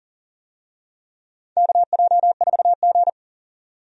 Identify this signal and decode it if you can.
Morse code, heard as a keyed tone, about 37 wpm, 700 Hz. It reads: KJ4G
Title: USA Call Signs - 4 Characters - 40wpm